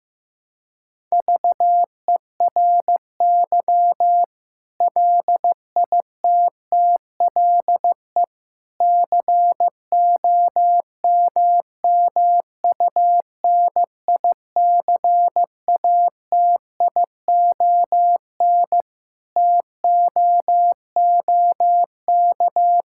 Morse code, heard as a keyed tone, about 16 wpm, 700 Hz. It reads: VERY LITTLE COMMUNICATION TOOK
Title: Sentences from Top 1000 Words - Review Part 6 - 15wpm